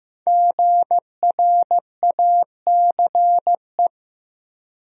Morse code, heard as a keyed tone, about 15 wpm, 700 Hz. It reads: GRACE
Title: USA Common Names - Max 5 Letters - 15wpm